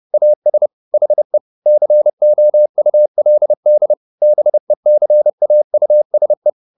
Morse code, heard as a keyed tone, about 30 wpm, 600 Hz. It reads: AS HE COULD BECAUSE